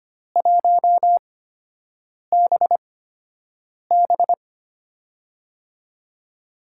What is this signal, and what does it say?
Morse code, heard as a keyed tone, about 25 wpm, spaced as for 7 wpm, 700 Hz. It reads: 1BB
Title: USA Call Signs - 4 Characters - 8x Character Spacing - 25wpm